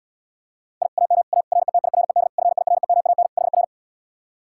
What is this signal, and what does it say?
Morse code, heard as a keyed tone, about 77 wpm, 700 Hz. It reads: I AM A RETIRED FIREMAN HR